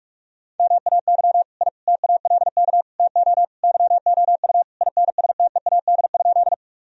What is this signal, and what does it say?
Morse code, heard as a keyed tone, about 44 wpm, 700 Hz. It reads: MAY I TALK TO YOU INSTEAD?
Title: Sentences from Top 1000 Words - Review Part 4 - 45wpm